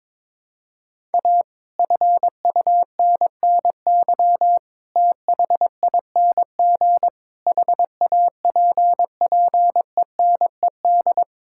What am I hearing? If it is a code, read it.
Morse code, heard as a keyed tone, about 21 wpm, 700 Hz. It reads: A FUNNY THING HAPPENED
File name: Sets of 9 Words - 22wpm